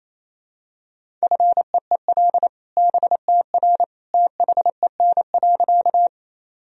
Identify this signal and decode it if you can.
Morse code, heard as a keyed tone, about 27 wpm, 700 Hz. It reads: FEEL BTR THEN.
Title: Sentences from Top 400 Words and Abbreviations 28wpm